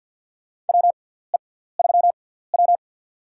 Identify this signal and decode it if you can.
Morse code, heard as a keyed tone, about 50 wpm, spaced as for 17 wpm, 700 Hz. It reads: WE3W
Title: USA Call Signs - 4 Characters - 6x Character Spacing - 50wpm